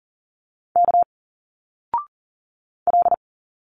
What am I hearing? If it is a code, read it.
Morse code, heard as a keyed tone, about 40 wpm, 700 Hz. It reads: K L
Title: Single Letters 40wpm